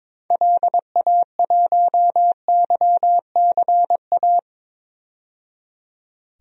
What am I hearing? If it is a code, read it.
Morse code, heard as a keyed tone, about 22 wpm, 700 Hz. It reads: LA1YCA
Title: DXCC-100 Call Signs - 6 Characters - 22wpm